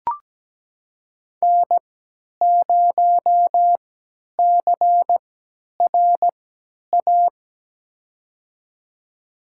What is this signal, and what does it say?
Morse code, heard as a keyed tone, about 17 wpm, spaced as for 10 wpm, 700 Hz. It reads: N0CRA
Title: USA Call Signs - 3x Character Spacing - 17wpm